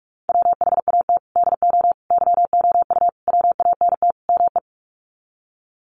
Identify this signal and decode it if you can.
Morse code, heard as a keyed tone, about 44 wpm, 700 Hz. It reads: WHAT DO YOU WANT ME
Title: Sets of 5 Words Encore2 45wpm